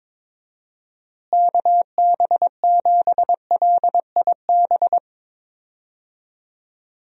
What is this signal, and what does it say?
Morse code, heard as a keyed tone, about 22 wpm, 700 Hz. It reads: KB7LIB
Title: USA Call Signs - 6 Characters - Repeated 2x - 22wpm